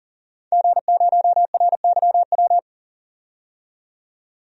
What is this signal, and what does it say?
Morse code, heard as a keyed tone, about 38 wpm, 700 Hz. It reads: G0RYW
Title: DXCC-100 Call Signs - 5 Characters - 40wpm